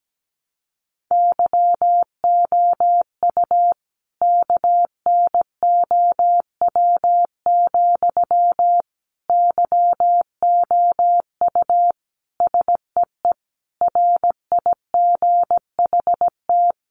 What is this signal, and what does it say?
Morse code, heard as a keyed tone, about 17 wpm, 700 Hz. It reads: YOU KNOW, YOU SEE RIGHT
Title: Sentences from Top 200 Words 17wpm